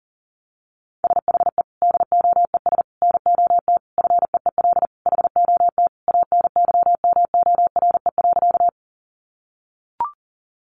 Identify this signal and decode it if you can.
Morse code, heard as a keyed tone, about 37 wpm, 700 Hz. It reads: SHE DOES NOT FEEL HOT ANYMORE.
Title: Additional Easy Phrases 40wpm